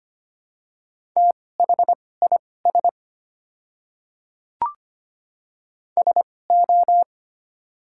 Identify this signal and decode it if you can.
Morse code, heard as a keyed tone, about 25 wpm, spaced as for 18 wpm, 700 Hz. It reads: THIS SO
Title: 2x Character Spacing - Top 200 Words - 25wpm